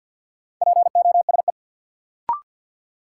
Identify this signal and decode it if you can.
Morse code, heard as a keyed tone, about 51 wpm, 700 Hz. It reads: POSE
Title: Sentences from Top 1000 Words - Review Part 2 - New Word Drill - 50wpm